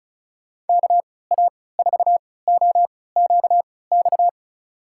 Morse code, heard as a keyed tone, about 36 wpm, spaced as for 20 wpm, 700 Hz. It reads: KA4OQX
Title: USA Call Signs - 6 Characters - 3x Character Spacing - 35wpm